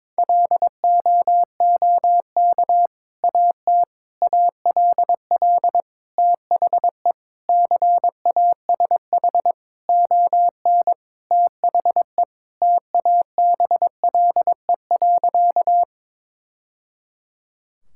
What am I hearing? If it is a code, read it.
Morse code, heard as a keyed tone, about 22 wpm, 700 Hz. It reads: LOOK AT ALL THE CASH ON THE TABLE.
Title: Sentences from Top 1000 Words - Review Part 5 - 22wpm